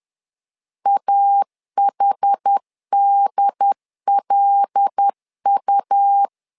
Morse code, heard as a keyed tone, about 10 wpm, 800 Hz. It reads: AHDLU
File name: LEZIONI / 5 LBVC / 5 L 10wpm 2259